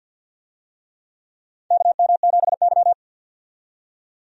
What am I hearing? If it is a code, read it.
Morse code, heard as a keyed tone, about 51 wpm, 700 Hz. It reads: KM7Y
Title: USA Call Signs - 4 Characters - 50wpm